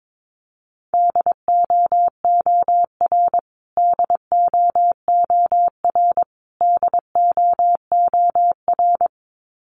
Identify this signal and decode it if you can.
Morse code, heard as a keyed tone, about 22 wpm, 700 Hz. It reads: DOOR DOOR DOOR
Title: Top 500 Words Repeated 22wpm